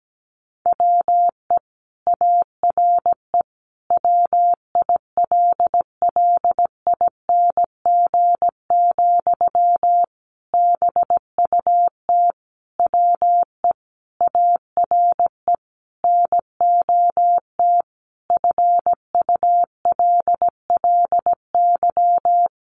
Morse code, heard as a keyed tone, about 17 wpm, 700 Hz. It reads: WE ARE WILLING, BUT WE ARE NOT FULLY
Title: Sentences from Top 1300 Words 17wpm